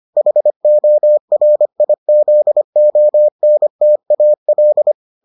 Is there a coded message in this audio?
Yes: Morse code from 0.2 to 4.9 s, heard as a keyed tone, about 25 wpm, 600 Hz. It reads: HORIZONTAL